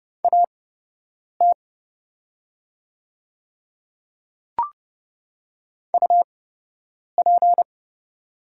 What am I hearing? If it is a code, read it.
Morse code, heard as a keyed tone, about 30 wpm, spaced as for 8 wpm, 700 Hz. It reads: AT UP
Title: 8x Character Spacing - Top 100 Words - 30wpm